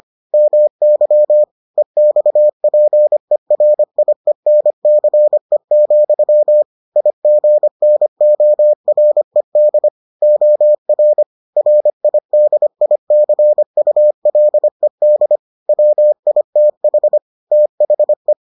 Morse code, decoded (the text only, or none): MY EXPERIENCE, IGNORED OR RIDICULED WITH THE